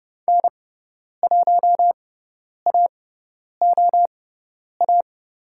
Morse code, heard as a keyed tone, about 30 wpm, spaced as for 10 wpm, 700 Hz. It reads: N1AOA
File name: course / 30:10wpm / 102 Lesson 17g - Letter u - Call Signs 30wpm